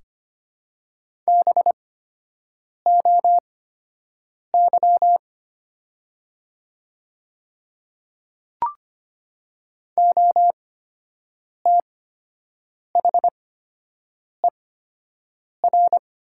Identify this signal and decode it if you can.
Morse code, heard as a keyed tone, about 25 wpm, spaced as for 7 wpm, 700 Hz. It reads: BOY OTHER
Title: 8x Character Spacing - Top 200 Words - 25wpm